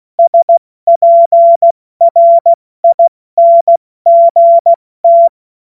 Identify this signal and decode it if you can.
Morse code, heard as a keyed tone, about 16 wpm, 680 Hz. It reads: SPRINGT
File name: Morsekurs / Witze / Witz 02 April 2024 70 F80